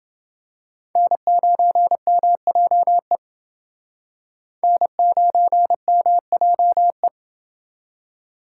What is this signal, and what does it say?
Morse code, heard as a keyed tone, about 29 wpm, 700 Hz. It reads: N9MJE N9MJE